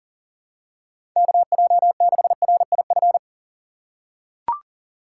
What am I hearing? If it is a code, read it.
Morse code, heard as a keyed tone, about 40 wpm, 700 Hz. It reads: KJ6RIF
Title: USA Call Signs - 6 Characters - 2x Character Spacing - 40wpm